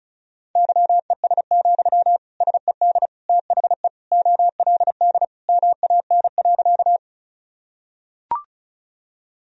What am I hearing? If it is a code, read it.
Morse code, heard as a keyed tone, about 36 wpm, 700 Hz. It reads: YES, SED THE OLD MAN.
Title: Sentences from Top 500 Words and Abbreviations 35wpm